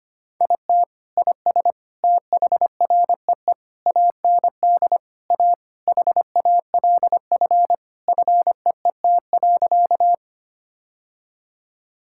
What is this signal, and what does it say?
Morse code, heard as a keyed tone, about 25 wpm, 700 Hz. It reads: IT IS THREE AND A HALF FEET.
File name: practice / 25wpm / Sentences with 3-Word Phrases - 25wpm